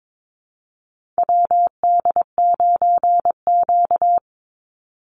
Morse code, heard as a keyed tone, about 22 wpm, 700 Hz. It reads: WD9Q
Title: Call Signs Repeated 3x - 22wpm